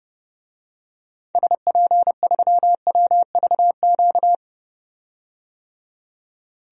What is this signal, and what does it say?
Morse code, heard as a keyed tone, about 30 wpm, 700 Hz. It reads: SP3WVQ
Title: DXCC-100 Call Signs - 6 Characters - 30wpm